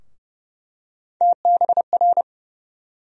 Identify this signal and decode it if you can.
Morse code, heard as a keyed tone, about 30 wpm, 700 Hz. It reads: TBR
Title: USA Call Sign Postfixes - 3 Letters 30wpm